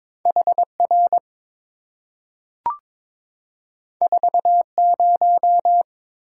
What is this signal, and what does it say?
Morse code, heard as a keyed tone, about 22 wpm, 700 Hz. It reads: HR 40
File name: practice / 22wpm / Two Letter-Number Combinations - 22wpm